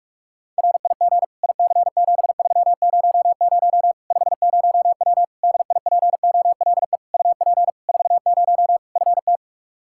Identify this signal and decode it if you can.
Morse code, heard as a keyed tone, about 46 wpm, 700 Hz. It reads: RIG IC7300 50W DIPOLE UP 40 FT
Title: Practice QSO - Exchange 2 - Info - 45wpm